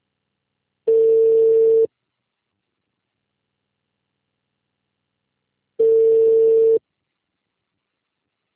Ringback tone (China, 450 Hz).
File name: out- -2026- - - .20